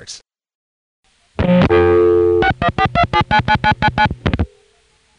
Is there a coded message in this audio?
Yes: DTMF from 2.4 to 4.1 s, heard as telephone touch tones, tones of about 70 ms. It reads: 6142799999